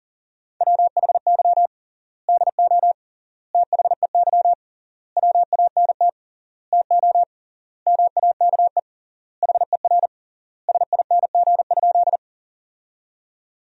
Morse code, heard as a keyed tone, about 40 wpm, 700 Hz. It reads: WHY DO THEY WANT TO MAKE HER SING?
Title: 3x Word-Spacing - Sentences from Top 400 Words - 40wpm